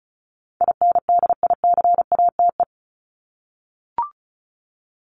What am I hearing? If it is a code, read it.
Morse code, heard as a keyed tone, about 34 wpm, 700 Hz. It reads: INDICATE
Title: Top 1251-1500 Words 35wpm